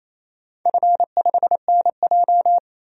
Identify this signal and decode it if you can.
Morse code, heard as a keyed tone, about 28 wpm, 700 Hz. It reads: F5NJ
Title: DXCC-100 Call Signs - 4 Characters - 28wpm